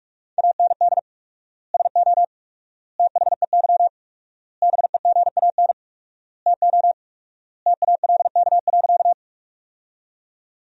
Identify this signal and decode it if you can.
Morse code, heard as a keyed tone, about 42 wpm, 700 Hz. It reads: AND SO THEY BEGAN TO TALK.
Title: 4x Word-Spacing - Sentences from Top 400 Words - 45wpm